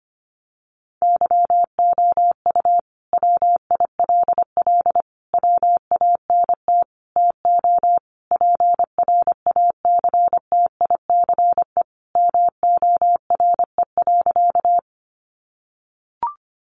Morse code, heard as a keyed tone, about 25 wpm, 700 Hz. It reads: YOU WILL WANT TO PRACTICE MORE.
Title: Sentences from Top 700 Words 25wpm